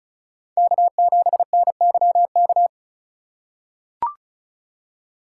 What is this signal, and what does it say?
Morse code, heard as a keyed tone, about 34 wpm, 700 Hz. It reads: K7NYK
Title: USA Call Signs - 5 Characters - 3x Character Spacing - 35wpm